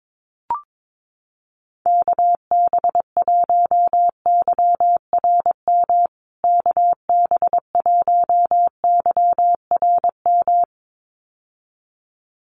Morse code, heard as a keyed tone, about 22 wpm, 700 Hz. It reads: KB1YRM KB1YRM
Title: USA Call Signs Repeated 2x 22wpm